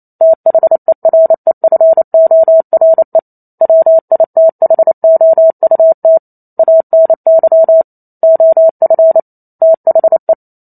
Morse code, heard as a keyed tone, about 28 wpm, 650 Hz. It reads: THEREFORE WITHOUT ANY OF THE